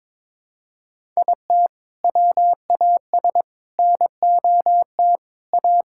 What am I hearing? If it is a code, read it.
Morse code, heard as a keyed tone, about 22 wpm, 700 Hz. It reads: IT WAS NOT A
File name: practice / 22wpm / Sentences with 4-Word Phrases - 22wpm